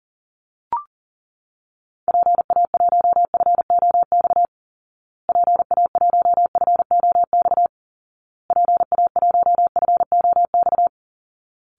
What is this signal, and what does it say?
Morse code, heard as a keyed tone, about 40 wpm, 700 Hz. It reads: PA1FOX PA1FOX PA1FOX
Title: Call Signs Repeated 2x - 40wpm